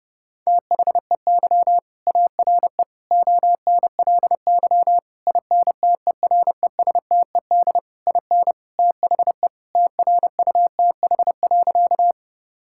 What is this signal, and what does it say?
Morse code, heard as a keyed tone, about 30 wpm, 700 Hz. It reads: THEY ARE ONLY INTERESTED IN THE TRUTH.
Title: Sentences from Top 1000 Words - Review Part 6 - 30wpm